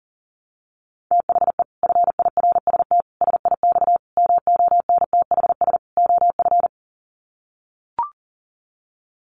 Morse code, heard as a keyed tone, about 37 wpm, 700 Hz. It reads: THE FIRST SIX MONTHS OF
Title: Sets of 5 Words Encore2 40wpm